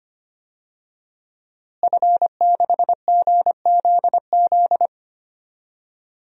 Morse code, heard as a keyed tone, about 25 wpm, 700 Hz. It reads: F6GZZ
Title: DXCC-100 Call Signs - 5 Characters - Repeated 3x - 25wpm